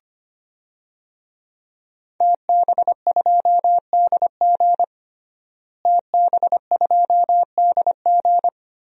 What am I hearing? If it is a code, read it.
Morse code, heard as a keyed tone, about 25 wpm, 700 Hz. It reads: TB2DG TB2DG
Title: DXCC-100 Call Signs - 5 Characters - Repeated 2x - 25wpm